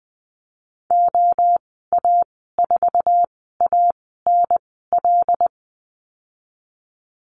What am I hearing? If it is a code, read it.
Morse code, heard as a keyed tone, about 20 wpm, spaced as for 14 wpm, 700 Hz. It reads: OA4ANL
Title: 082 Lesson 14g - Number 2 - Call Signs 20wpm